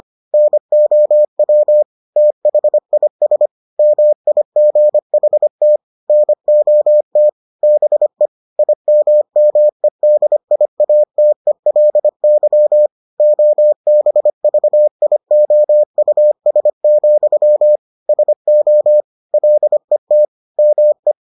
Morse code, heard as a keyed tone, about 25 wpm, 600 Hz. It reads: NOW THIS MIGHT NOT BE IMMEDIATELY OBVIOUS, SO LET ME